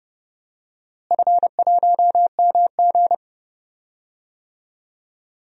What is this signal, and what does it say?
Morse code, heard as a keyed tone, about 32 wpm, 700 Hz. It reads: F1MG
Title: DXCC-100 Call Signs - 4 Characters - Repeated 3x - 30wpm